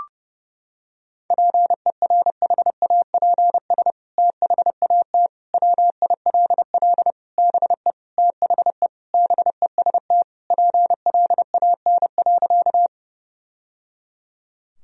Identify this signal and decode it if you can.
Morse code, heard as a keyed tone, about 30 wpm, 700 Hz. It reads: PERHAPS THAT WILL BE THE BEST PLAN.
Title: Sentences from Top 1000 Words - Review Part 2 - 30wpm